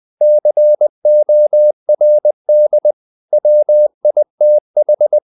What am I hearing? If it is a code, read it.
Morse code, heard as a keyed tone, about 20 wpm, 600 Hz. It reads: CORD WITH
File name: the old man and the sea / the old man and the sea part 1 20wpm